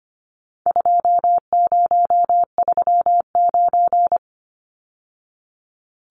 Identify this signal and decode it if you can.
Morse code, heard as a keyed tone, about 25 wpm, 700 Hz. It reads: 2039
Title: Numbers - 4-Digit 25wpm